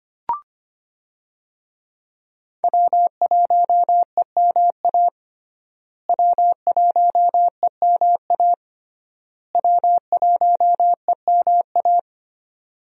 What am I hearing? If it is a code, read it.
Morse code, heard as a keyed tone, about 25 wpm, 700 Hz. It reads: W1EMA W1EMA W1EMA
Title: USA Call Signs - 5 Characters - Repeated 3x - 25wpm